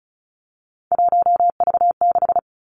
Morse code, heard as a keyed tone, about 36 wpm, 700 Hz. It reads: 1V6
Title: Three Challenging Letter-Number Combinations 35wpm